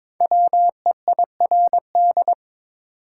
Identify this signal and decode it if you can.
Morse code, heard as a keyed tone, about 22 wpm, 700 Hz. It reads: WEIRD
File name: 5 Letter Words - First 3 Letters Given - 22wpm